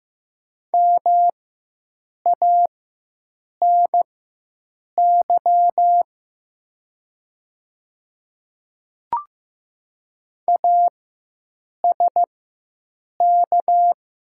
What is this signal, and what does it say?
Morse code, heard as a keyed tone, about 15 wpm, spaced as for 7 wpm, 700 Hz. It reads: MANY ASK
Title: 4x Character Spacing - Top 200 Words - 15wpm